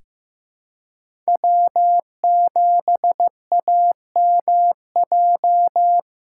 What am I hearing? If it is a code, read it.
Morse code, heard as a keyed tone, about 15 wpm, 700 Hz. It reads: W7AMJ
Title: USA Call Signs - 5 Characters - 3x Character Spacing - 15wpm